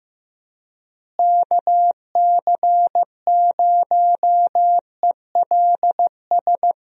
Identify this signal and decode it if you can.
Morse code, heard as a keyed tone, about 15 wpm, 700 Hz. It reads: KC0ELS
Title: USA Call Signs - 6 Characters - Repeated 3x - 15wpm